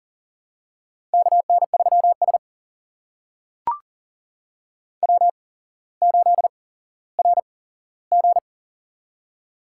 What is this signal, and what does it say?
Morse code, heard as a keyed tone, about 42 wpm, spaced as for 17 wpm, 700 Hz. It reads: KN3S W8RG
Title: USA Call Signs - 4 Characters - 8x Character Spacing - 40wpm